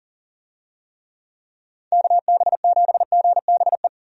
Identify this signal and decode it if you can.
Morse code, heard as a keyed tone, about 40 wpm, 700 Hz. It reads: KB7GBE
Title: USA Call Signs - 6 Characters - 40wpm